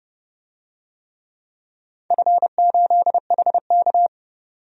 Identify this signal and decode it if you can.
Morse code, heard as a keyed tone, about 30 wpm, 700 Hz. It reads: F8HK
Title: DXCC-100 Call Signs - 4 Characters - 30wpm